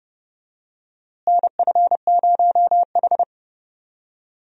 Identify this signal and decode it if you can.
Morse code, heard as a keyed tone, about 32 wpm, 700 Hz. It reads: NF0H
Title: USA Call Signs - 4 Characters - Repeated 2x - 30wpm